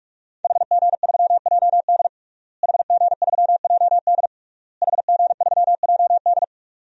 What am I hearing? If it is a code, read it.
Morse code, heard as a keyed tone, about 42 wpm, 700 Hz. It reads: HG3JD HG3JD HG3JD
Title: DXCC-100 Call Signs - 5 Characters - Repeated 3x - 45wpm